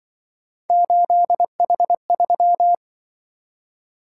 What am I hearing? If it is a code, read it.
Morse code, heard as a keyed tone, about 24 wpm, 700 Hz. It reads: 8H3
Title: Speed-Racing - Three Letter-Number Combinations - 20wpm